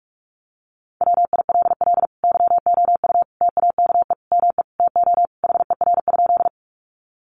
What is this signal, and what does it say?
Morse code, heard as a keyed tone, about 44 wpm, 700 Hz. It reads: WILL YOU TAKE ME TO HER?
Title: Sentences from Top 200 Words 45wpm